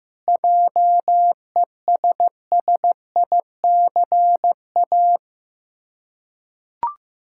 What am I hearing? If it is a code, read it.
Morse code, heard as a keyed tone, about 16 wpm, 700 Hz. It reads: JESSICA
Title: Practice QSO - Exchange 1 - Beginning - 15wpm